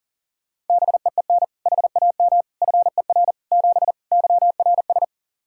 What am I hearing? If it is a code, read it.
Morse code, heard as a keyed tone, about 40 wpm, 700 Hz. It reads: BEEN HAM FER 7 YRS
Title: K1USN SST QSO Practice 40wpm